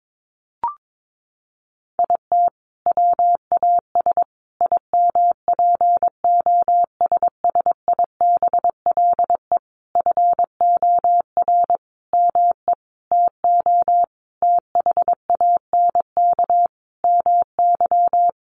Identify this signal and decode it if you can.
Morse code, heard as a keyed tone, about 22 wpm, 700 Hz. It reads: IT WAS IMPOSSIBLE FOR ME TO THANK MY
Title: Sentences from Top 1100 Words 22wpm